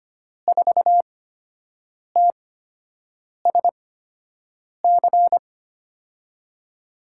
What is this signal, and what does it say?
Morse code, heard as a keyed tone, about 25 wpm, spaced as for 7 wpm, 700 Hz. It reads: 4TSC
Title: USA Call Signs - 6 Characters - 8x Character Spacing - 25wpm